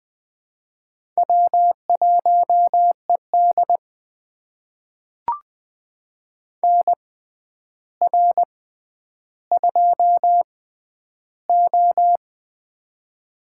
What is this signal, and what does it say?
Morse code, heard as a keyed tone, about 21 wpm, spaced as for 10 wpm, 700 Hz. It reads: W1ED NR2O
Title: USA Call Signs - 6x Character Spacing - 20wpm